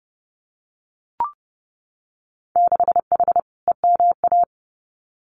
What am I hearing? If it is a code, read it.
Morse code, heard as a keyed tone, about 30 wpm, 700 Hz. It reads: 6H EMA
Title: Winter Field Day Exchanges Encore 30wpm